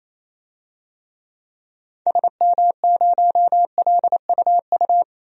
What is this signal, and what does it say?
Morse code, heard as a keyed tone, about 28 wpm, 700 Hz. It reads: SM0LUU
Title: DXCC-100 Call Signs - 6 Characters - 28wpm